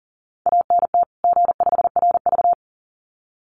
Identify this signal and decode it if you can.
Morse code, heard as a keyed tone, about 40 wpm, 700 Hz. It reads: ANT G5RV
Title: Common QSO Phrases 40wpm